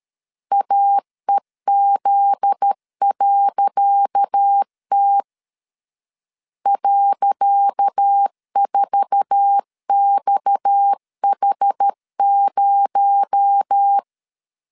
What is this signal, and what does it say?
Morse code, heard as a keyed tone, about 13 wpm, 800 Hz. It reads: AEZ.T .4XH0
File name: LEZIONI / 10 SIGNS / 10C DOT 12wpm 2451